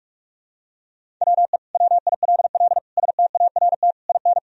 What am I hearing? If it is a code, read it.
Morse code, heard as a keyed tone, about 47 wpm, 700 Hz. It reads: WE WILL START IN
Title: Sentences from Top 1000 Words - Review Part 1 - 45wpm